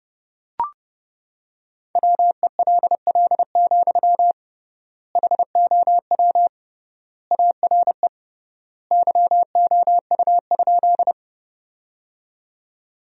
Morse code, heard as a keyed tone, about 30 wpm, 700 Hz. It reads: WELL, HOW ARE YOU?
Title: 3x Word-Spacing - Sentences from Top 200 Words - 30wpm